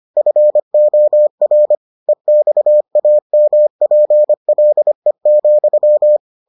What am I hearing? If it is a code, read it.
Morse code, heard as a keyed tone, about 25 wpm, 600 Hz. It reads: FOR EXAMPLE,